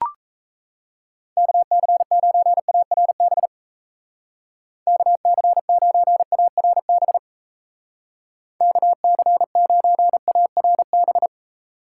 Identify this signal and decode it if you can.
Morse code, heard as a keyed tone, about 34 wpm, 700 Hz. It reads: KC9ARB KC9ARB KC9ARB
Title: Speed Racing - USA Call Signs 28wpm